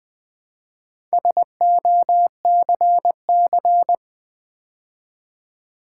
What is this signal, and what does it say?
Morse code, heard as a keyed tone, about 20 wpm, 700 Hz. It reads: SOCC
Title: Build A Word - 2 Letters At A Time - 6-Letter Words - 20wpm